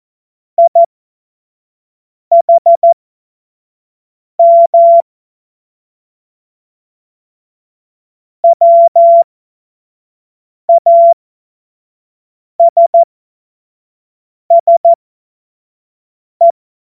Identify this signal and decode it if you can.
Morse code, heard as a keyed tone, about 14 wpm, spaced as for 5 wpm, 680 Hz. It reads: IHM WASSE
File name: Morsekurs / Witze / Witz 14. März 2023 25 F70